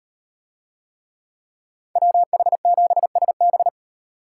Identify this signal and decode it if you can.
Morse code, heard as a keyed tone, about 38 wpm, 700 Hz. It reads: WH7SB